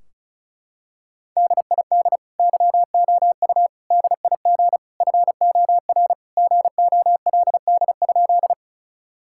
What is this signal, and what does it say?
Morse code, heard as a keyed tone, about 36 wpm, 700 Hz. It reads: DID YOU DIG FOR GOLD?
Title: Sentences from Top 1600 Words 35wpm